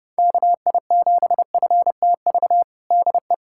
Morse code, heard as a keyed tone, about 30 wpm, 700 Hz. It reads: KI7FTV DE
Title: USA Callsign DE Callsign - 6 Characters - 30wpm